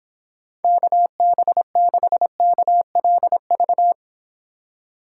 Morse code, heard as a keyed tone, about 28 wpm, 700 Hz. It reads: KB6KLV